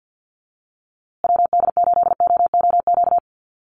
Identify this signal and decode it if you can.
Morse code, heard as a keyed tone, about 50 wpm, 700 Hz. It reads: WD8OOQ
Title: USA Call Signs Repeated 2x 50wpm